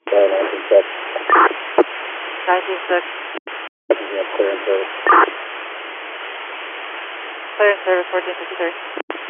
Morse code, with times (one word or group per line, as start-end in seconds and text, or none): none